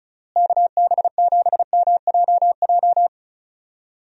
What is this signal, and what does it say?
Morse code, heard as a keyed tone, about 34 wpm, 700 Hz. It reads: KB7MJJ
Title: USA Call Signs - 6 Characters - Repeated 3x - 35wpm